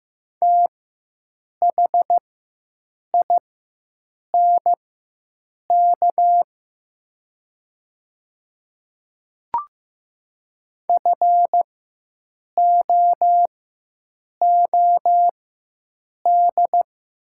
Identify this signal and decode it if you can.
Morse code, heard as a keyed tone, about 15 wpm, spaced as for 7 wpm, 700 Hz. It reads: THINK FOOD
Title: 4x Character Spacing - Top 300 Words - 15wpm